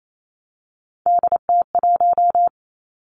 Morse code, heard as a keyed tone, about 28 wpm, 700 Hz. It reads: DT1
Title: Three Challenging Letter-Number Combinations 28wpm